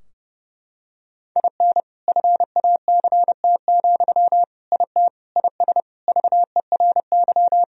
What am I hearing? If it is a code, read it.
Morse code, heard as a keyed tone, about 32 wpm, 700 Hz. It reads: IN FACT, IT IS VERY
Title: Sentences from Top 1000 Words - Review Part 3 - 30wpm